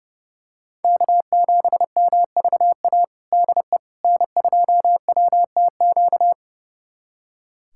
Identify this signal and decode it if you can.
Morse code, heard as a keyed tone, about 30 wpm, 700 Hz. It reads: K7MVA DE N2WTQ
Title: USA Callsign DE Callsign - 5 Characters - 30wpm